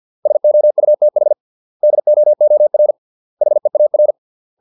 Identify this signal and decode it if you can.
Morse code, heard as a keyed tone, about 50 wpm, 600 Hz. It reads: SOUTH DOOR HEAR